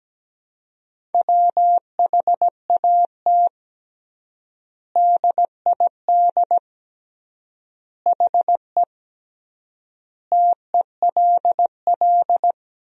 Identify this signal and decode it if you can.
Morse code, heard as a keyed tone, about 17 wpm, 700 Hz. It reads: WHAT DID HE TELL
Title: 3x Word-Spacing - Sentences from Top 200 Words - 17wpm